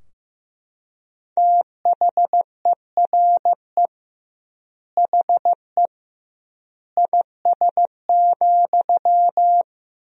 Morse code, heard as a keyed tone, about 16 wpm, 700 Hz. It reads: THERE HE IS,
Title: 2x Word-Spacing - Sentences from Top 100 Words - 15wpm